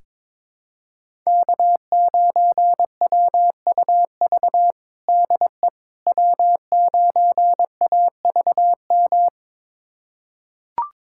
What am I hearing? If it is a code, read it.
Morse code, heard as a keyed tone, about 22 wpm, 700 Hz. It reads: K9WUV DE W9AVM
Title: USA Callsign DE Callsign - 5 Characters - 22wpm